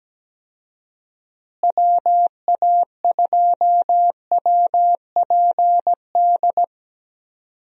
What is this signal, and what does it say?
Morse code, heard as a keyed tone, about 17 wpm, 700 Hz. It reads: WA2WPD
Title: USA Call Signs - 6 Characters - Repeated 2x - 17wpm